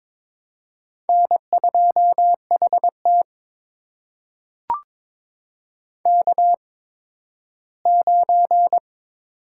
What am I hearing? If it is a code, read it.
Morse code, heard as a keyed tone, about 22 wpm, spaced as for 13 wpm, 700 Hz. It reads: N2HT K9
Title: USA Call Signs - 8x Character Spacing - 22wpm